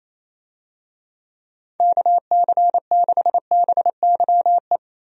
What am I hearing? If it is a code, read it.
Morse code, heard as a keyed tone, about 28 wpm, 700 Hz. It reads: KC6BYE
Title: USA Call Signs - 6 Characters - Repeated 3x - 28wpm